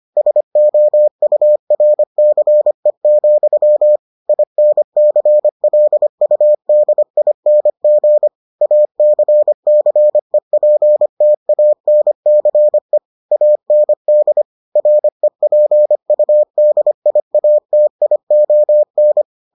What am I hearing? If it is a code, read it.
Morse code, heard as a keyed tone, about 25 wpm, 600 Hz. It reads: SOURCE, INCLUDING ACCEPTANCE AND REPUDIATION